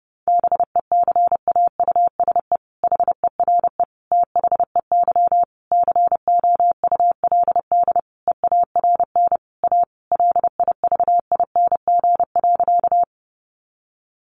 Morse code, heard as a keyed tone, about 30 wpm, 700 Hz. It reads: BECAUSE HERE THEY COULD EARN A LIVING.
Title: Sentences from Top 1500 Words 30wpm